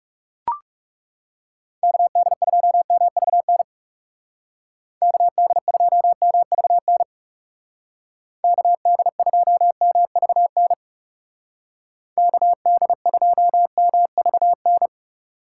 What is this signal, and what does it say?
Morse code, heard as a keyed tone, about 39 wpm, 700 Hz. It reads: KD2MVN KD2MVN KD2MVN KD2MVN